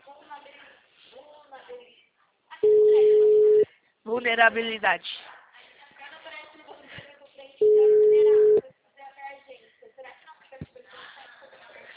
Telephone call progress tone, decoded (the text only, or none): ringback tone